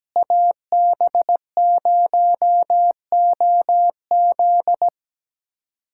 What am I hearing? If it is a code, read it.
Morse code, heard as a keyed tone, about 17 wpm, 700 Hz. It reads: AB0OZ
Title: USA Callsign DE Callsign - 5 Characters - 17wpm